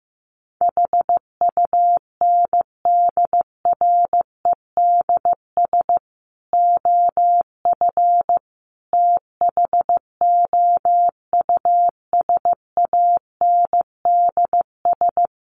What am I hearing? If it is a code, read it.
Morse code, heard as a keyed tone, about 14 wpm, 700 Hz. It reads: HUNDREDS OF THOUSANDS
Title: Sets of 5 Words Encore2 15wpm